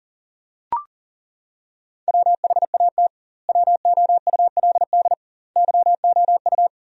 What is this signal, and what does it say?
Morse code, heard as a keyed tone, about 37 wpm, 700 Hz. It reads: WHAT WOULD YOU
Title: 2x Word-Spacing - Sentences from Top 300 Words - 40wpm